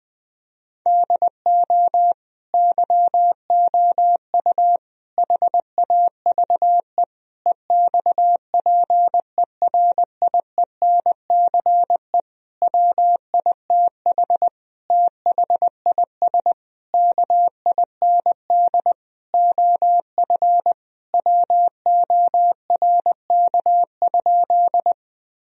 Morse code, heard as a keyed tone, about 20 wpm, 700 Hz. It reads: DO YOU HAVE EXPERIENCE WITH THIS KIND OF WORK?
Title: Sentences from Top 1000 Words - Review Part 3 - 20wpm